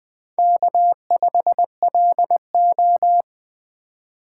Morse code, heard as a keyed tone, about 20 wpm, 700 Hz. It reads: K5LO
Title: USA Call Signs - 2x Character Spacing - 20wpm